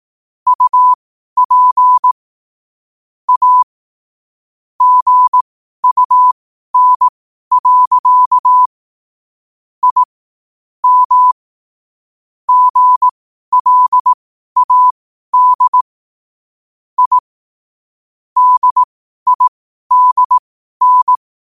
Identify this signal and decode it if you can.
Morse code, heard as a keyed tone, about 18 wpm, spaced as for 12 wpm, 1000 Hz. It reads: UP A GUN. I M GLAD I DIDN
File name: qotdfiles / 12wpm